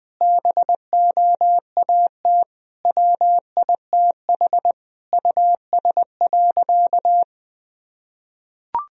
Morse code, heard as a keyed tone, about 20 wpm, 700 Hz. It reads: BOAT WITH US.
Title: Sentences from Top 1000 Words - Review Part 3 - 20wpm